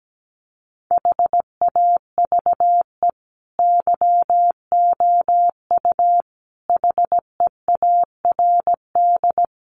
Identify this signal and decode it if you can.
Morse code, heard as a keyed tone, about 17 wpm, 700 Hz. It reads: HAVE YOU HEARD
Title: Sentences from Top 1000 Words 17wpm